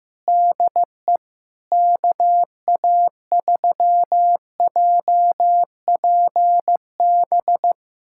Morse code, heard as a keyed tone, about 15 wpm, 700 Hz. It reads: DE KA3JPB
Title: USA Callsign DE Callsign - 6 Characters - 15wpm